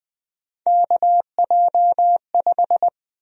Morse code, heard as a keyed tone, about 20 wpm, 700 Hz. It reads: KJ5
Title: USA Call Sign Prefixes - 20wpm